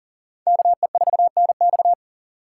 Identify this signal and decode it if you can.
Morse code, heard as a keyed tone, about 51 wpm, 700 Hz. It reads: KE4NX